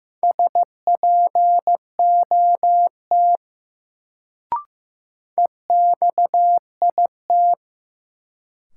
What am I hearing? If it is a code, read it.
Morse code, heard as a keyed tone, about 16 wpm, 700 Hz. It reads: SPOT EXIT
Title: Sentences from Top 1100 Words - New Word Drill Alt - 15wpm